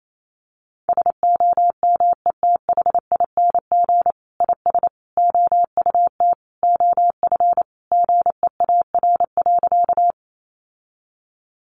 Morse code, heard as a keyed tone, about 28 wpm, 700 Hz. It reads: SOMETHING IS OUT OF GEAR.
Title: Sentences from Top 1400 Words 28wpm